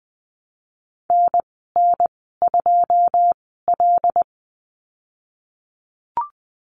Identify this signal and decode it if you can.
Morse code, heard as a keyed tone, about 20 wpm, spaced as for 15 wpm, 700 Hz. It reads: NN2L
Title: 082 Lesson 14g - Number 2 - Call Signs 20wpm